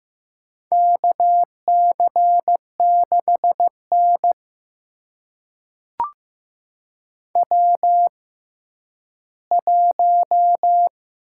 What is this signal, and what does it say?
Morse code, heard as a keyed tone, about 15 wpm, spaced as for 10 wpm, 700 Hz. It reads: KC6N W1
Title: USA Call Signs - 6x Character Spacing - 15wpm